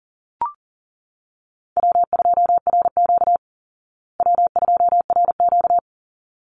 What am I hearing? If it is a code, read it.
Morse code, heard as a keyed tone, about 40 wpm, 700 Hz. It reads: W2RQ W2RQ
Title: Call Signs Repeated 3x - 40wpm